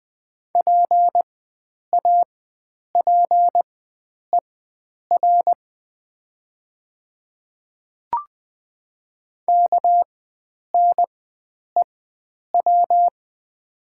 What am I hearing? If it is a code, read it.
Morse code, heard as a keyed tone, about 20 wpm, spaced as for 9 wpm, 700 Hz. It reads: PAPER KNEW
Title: 4x Character Spacing - Top 500 Words - 20wpm